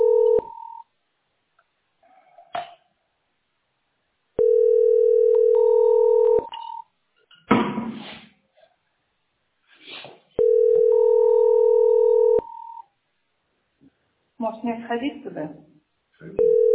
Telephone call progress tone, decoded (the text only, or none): ringback tone